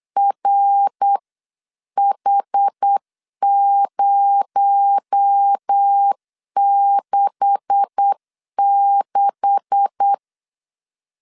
Morse code, heard as a keyed tone, about 8 wpm, 800 Hz. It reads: RH066
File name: LEZIONI / 9 286 8wpm 3425